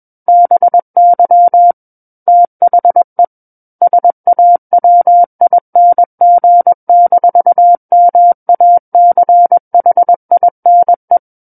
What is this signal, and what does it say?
Morse code, heard as a keyed tone, about 21 wpm, 700 Hz. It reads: BY THE SAWING-MACHINE
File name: Book One / 12-Chapter 12 - 21 WPM